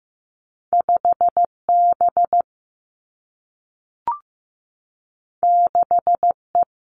Morse code, heard as a keyed tone, about 15 wpm, 700 Hz. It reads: 5 B 6 E
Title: Two Challenging Letter-Number Combinations 15wpm